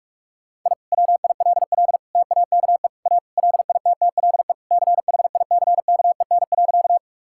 Morse code, heard as a keyed tone, about 46 wpm, 700 Hz. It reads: I WILL TAKE A LITTLE CHICKEN.
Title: Sentences from Top 1000 Words - Review Part 5 - 45wpm